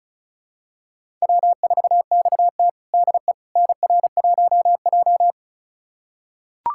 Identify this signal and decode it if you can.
Morse code, heard as a keyed tone, about 36 wpm, 700 Hz. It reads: W4XT DE NR1J
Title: USA Callsign DE Callsign - 4 Characters - 35wpm